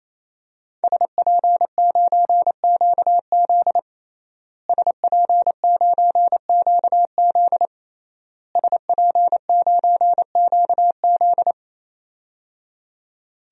Morse code, heard as a keyed tone, about 27 wpm, 700 Hz. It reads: SP9QZ SP9QZ SP9QZ
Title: DXCC-100 Call Signs - 5 Characters - Repeated 3x - 28wpm